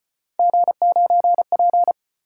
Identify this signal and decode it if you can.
Morse code, heard as a keyed tone, about 33 wpm, 700 Hz. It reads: G9P